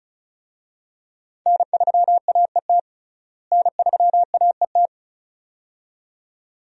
Morse code, heard as a keyed tone, about 34 wpm, 700 Hz. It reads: N3AET N3AET
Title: USA Call Signs - 5 Characters - Repeated 2x - 35wpm